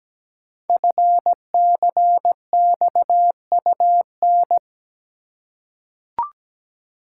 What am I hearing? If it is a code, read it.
Morse code, heard as a keyed tone, about 17 wpm, 700 Hz. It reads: FCXUN
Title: Code Groups - 5 Letters - 17wpm